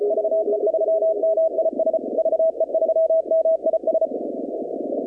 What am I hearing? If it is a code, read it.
Morse code, heard as a keyed tone, about 35 wpm, 660 Hz, with noise. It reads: VE3MIS VE3MIS
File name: contests / 2019-cw-wpx-kt5j-k5pi- -2330 0-21